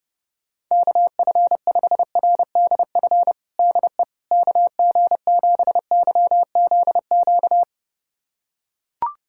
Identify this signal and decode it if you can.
Morse code, heard as a keyed tone, about 30 wpm, 700 Hz. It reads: KF5RDF DE KG7YZQ
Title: USA Callsign DE Callsign - 6 Characters - 30wpm